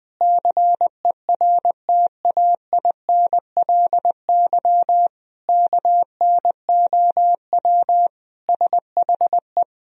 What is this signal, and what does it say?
Morse code, heard as a keyed tone, about 20 wpm, 700 Hz. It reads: CERTAINLY KNOW SHE
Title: Sentences from Top 1000 Words - Review Part 4 - 20wpm